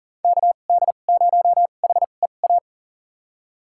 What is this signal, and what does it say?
Morse code, heard as a keyed tone, about 40 wpm, spaced as for 29 wpm, 700 Hz. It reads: KD0HEA
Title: USA Call Signs - 6 Characters - 2x Character Spacing - 40wpm